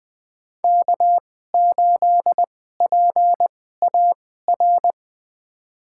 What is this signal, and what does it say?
Morse code, heard as a keyed tone, about 20 wpm, spaced as for 14 wpm, 700 Hz. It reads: K8PAR
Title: USA Call Signs - 5 Characters - 2x Character Spacing - 20wpm